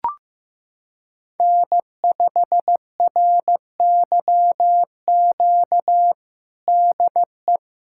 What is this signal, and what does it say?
Morse code, heard as a keyed tone, about 16 wpm, 700 Hz. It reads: N5RYQ DE
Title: USA Callsign DE Callsign - 5 Characters - 15wpm